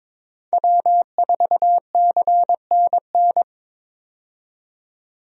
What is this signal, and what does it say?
Morse code, heard as a keyed tone, about 22 wpm, 700 Hz. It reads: W4CNN
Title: USA Call Signs - 5 Characters - 22wpm